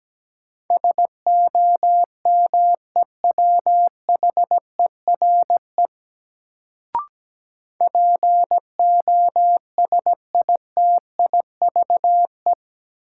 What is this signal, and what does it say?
Morse code, heard as a keyed tone, about 17 wpm, 700 Hz. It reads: SOMEWHERE POSITIVE
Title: Sentences from Top 1400 Words - New Word Drill Alt - 17wpm